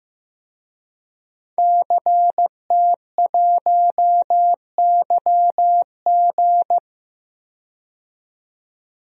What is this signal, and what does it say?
Morse code, heard as a keyed tone, about 15 wpm, 700 Hz. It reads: CT1YG
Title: DXCC-100 Call Signs - 5 Characters - 15wpm